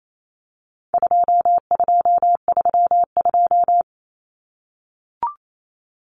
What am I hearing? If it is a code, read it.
Morse code, heard as a keyed tone, about 28 wpm, 700 Hz. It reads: 2232
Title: Common Cut Numbers - 4-Digit 28wpm